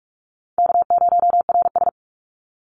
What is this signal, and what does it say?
Morse code, heard as a keyed tone, about 46 wpm, 700 Hz. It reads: K0RS
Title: NA SOTA Chaser Callsigns 45wpm